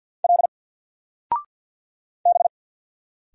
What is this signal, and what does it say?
Morse code, heard as a keyed tone, about 50 wpm, 700 Hz. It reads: L B
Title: Single Letter-Number 50wpm